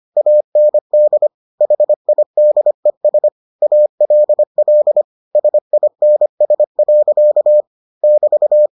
Morse code, heard as a keyed tone, about 25 wpm, 600 Hz. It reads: AND HIDES ALL SINS. <BT>